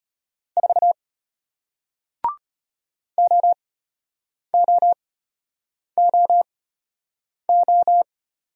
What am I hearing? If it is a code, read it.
Morse code, heard as a keyed tone, about 34 wpm, spaced as for 8 wpm, 700 Hz. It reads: 4 OOOO